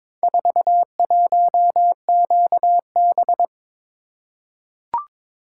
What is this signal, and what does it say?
Morse code, heard as a keyed tone, about 22 wpm, 700 Hz. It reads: 41QB
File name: practice / 22wpm / Four Letter-Number Combinations - 22wpm